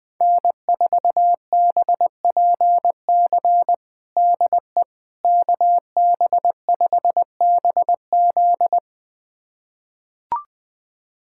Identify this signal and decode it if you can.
Morse code, heard as a keyed tone, about 20 wpm, 700 Hz. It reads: N4BPC DE KB5BZ
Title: USA Callsign DE Callsign - 5 Characters - 20wpm